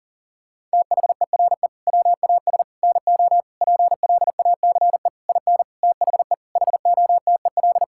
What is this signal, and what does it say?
Morse code, heard as a keyed tone, about 40 wpm, 700 Hz. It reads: THERE WAS NO PLACE IN THE HOTEL
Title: Sentences from Top 1600 Words 40wpm